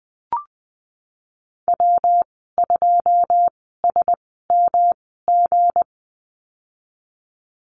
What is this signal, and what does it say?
Morse code, heard as a keyed tone, about 20 wpm, spaced as for 15 wpm, 700 Hz. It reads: W2SMG
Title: 186 Lesson 30g - Letter b - Call Signs 20wpm